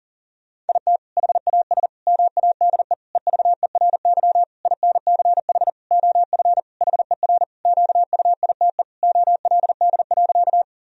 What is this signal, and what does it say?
Morse code, heard as a keyed tone, about 40 wpm, 700 Hz. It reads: IT HAS MADE EVERY INCH OF HER QUITE OLD.
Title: Sentences from Top 1000 Words - Review Part 3 - 40wpm